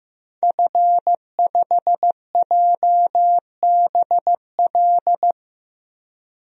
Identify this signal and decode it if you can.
Morse code, heard as a keyed tone, about 15 wpm, 700 Hz. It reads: F5JBL
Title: DXCC-100 Call Signs - 5 Characters - 15wpm